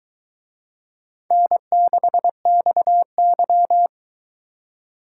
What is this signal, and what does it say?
Morse code, heard as a keyed tone, about 20 wpm, 700 Hz. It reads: N6XY